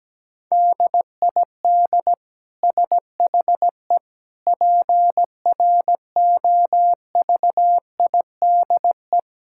Morse code, heard as a keyed tone, about 17 wpm, 700 Hz. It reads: DID SHE PROVIDE
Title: Sentences from Top 1000 Words - Review Part 5 - 17wpm